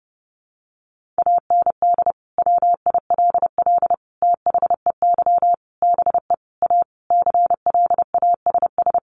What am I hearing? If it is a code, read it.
Morse code, heard as a keyed tone, about 30 wpm, 700 Hz. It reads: AND WILL THEY BE A CLASS
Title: Sentences from Top 1400 Words 30wpm